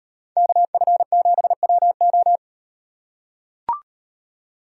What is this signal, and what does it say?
Morse code, heard as a keyed tone, about 38 wpm, 700 Hz. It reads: KF7WO